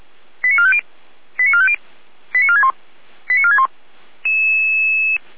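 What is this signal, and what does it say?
Five-tone selective call (ZVEI), heard as quick runs of five tones, tones of about 70 ms. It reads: 89450 89450 89461 89461
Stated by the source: ZVEI five-tone selective call 88355 received on an SDR.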